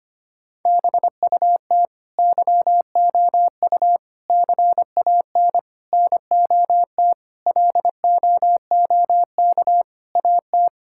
Morse code, heard as a keyed tone, about 25 wpm, 700 Hz. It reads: BUT YOU CAN NOT LOOK AT
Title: Sentences from Top 1000 Words - Review Part 6 - 25wpm